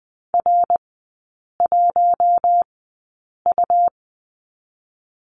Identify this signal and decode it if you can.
Morse code, heard as a keyed tone, about 20 wpm, spaced as for 8 wpm, 700 Hz. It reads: R1U
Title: 172 Lesson 28e - Number 9 - Three Characters 20wpm